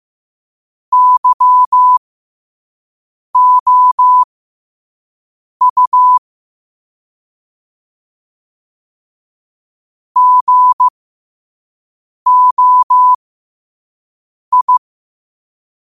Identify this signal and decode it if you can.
Morse code, heard as a keyed tone, about 15 wpm, spaced as for 5 wpm, 1000 Hz. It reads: YOU GOI
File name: qotdfiles / 5wpm